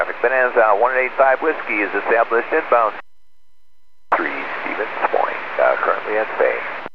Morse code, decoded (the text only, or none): none